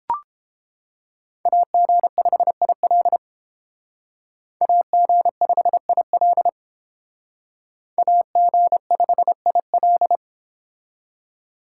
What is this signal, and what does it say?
Morse code, heard as a keyed tone, about 29 wpm, 700 Hz. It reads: AG5IL AG5IL AG5IL